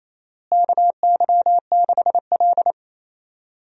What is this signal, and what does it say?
Morse code, heard as a keyed tone, about 27 wpm, 700 Hz. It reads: KY6L
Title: USA Call Signs - 4 Characters - Repeated 2x - 28wpm